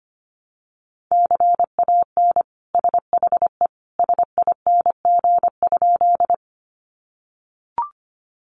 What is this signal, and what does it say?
Morse code, heard as a keyed tone, about 25 wpm, 700 Hz. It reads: CAN SHE SING?
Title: Sentences from Top 400 Words 25wpm